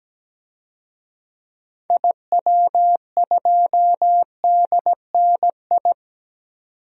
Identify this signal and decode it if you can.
Morse code, heard as a keyed tone, about 17 wpm, 700 Hz. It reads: IW2DNI
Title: DXCC-100 Call Signs - 6 Characters - 17wpm